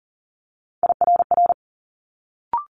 Morse code, heard as a keyed tone, about 40 wpm, 700 Hz. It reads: IRR
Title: Multi-letter Phonemes Quick Fire 40wpm